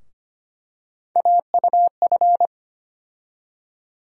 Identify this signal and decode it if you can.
Morse code, heard as a keyed tone, about 25 wpm, 700 Hz. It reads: AUF
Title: USA Call Sign Postfixes - 3 Letters 25wpm